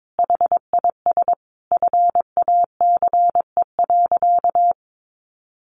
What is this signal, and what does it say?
Morse code, heard as a keyed tone, about 22 wpm, 700 Hz. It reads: HIS FACE.
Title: Sentences from Top 700 Words 22wpm